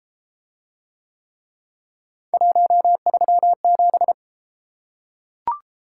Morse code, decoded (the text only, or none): 137